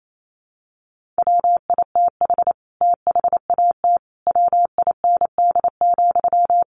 Morse code, heard as a keyed tone, about 28 wpm, 700 Hz. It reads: WITH THAT WIND,
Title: Sentences from Top 1400 Words 28wpm